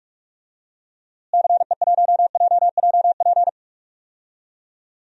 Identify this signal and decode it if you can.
Morse code, heard as a keyed tone, about 45 wpm, 700 Hz. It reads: CE1JJP
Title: DXCC-100 Call Signs - 6 Characters - Repeated 2x - 45wpm